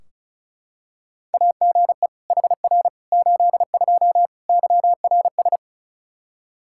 Morse code, heard as a keyed tone, about 36 wpm, 700 Hz. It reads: AGE HR 82 YRS
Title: K1USN SST QSO Practice 35wpm